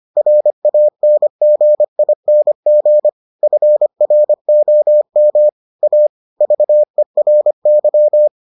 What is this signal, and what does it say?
Morse code, heard as a keyed tone, about 25 wpm, 600 Hz. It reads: RANGING FROM A VERY